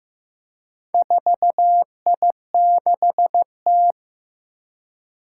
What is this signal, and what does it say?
Morse code, heard as a keyed tone, about 15 wpm, 700 Hz. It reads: 4I6T
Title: Four Challenging Letter-Number Combinations 15wpm